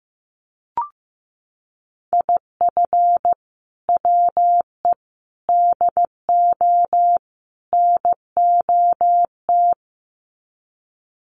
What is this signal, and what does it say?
Morse code, heard as a keyed tone, about 15 wpm, 700 Hz. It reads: IF WE DO NOT
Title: Sets of 4 Words 15wpm